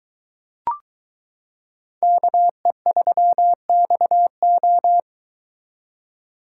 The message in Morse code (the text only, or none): KE3XO